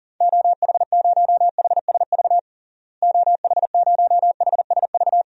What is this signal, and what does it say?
Morse code, heard as a keyed tone, about 40 wpm, 700 Hz. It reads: OH0HSV OH0HSV
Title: DXCC-100 Call Signs - 6 Characters - Repeated 2x - 40wpm